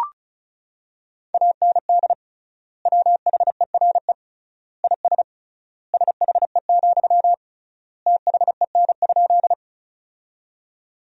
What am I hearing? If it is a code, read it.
Morse code, heard as a keyed tone, about 36 wpm, 700 Hz. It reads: AND WHERE IS SHE, THEN?
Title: 3x Word-Spacing - Sentences from Top 200 Words - 35wpm